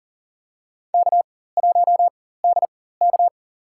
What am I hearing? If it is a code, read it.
Morse code, heard as a keyed tone, about 42 wpm, spaced as for 19 wpm, 700 Hz. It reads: K1DK
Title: USA Call Signs - 4 Characters - 4x Character Spacing - 40wpm